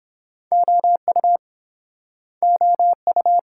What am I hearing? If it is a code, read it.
Morse code, heard as a keyed tone, about 29 wpm, 700 Hz. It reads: OU OU